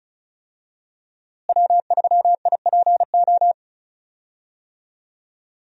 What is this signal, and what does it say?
Morse code, heard as a keyed tone, about 34 wpm, 700 Hz. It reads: W3IPO
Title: USA Call Signs - 5 Characters - Repeated 3x - 35wpm